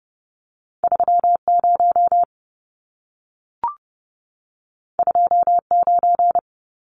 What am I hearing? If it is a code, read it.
Morse code, heard as a keyed tone, about 30 wpm, 700 Hz. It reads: 30 29
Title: Numbers - 2-Digit 30wpm